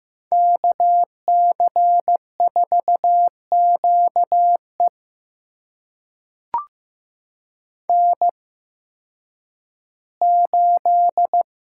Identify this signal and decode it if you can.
Morse code, heard as a keyed tone, about 15 wpm, spaced as for 10 wpm, 700 Hz. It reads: KC4QE N8
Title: USA Call Signs - 5 Characters - 8x Character Spacing - 15wpm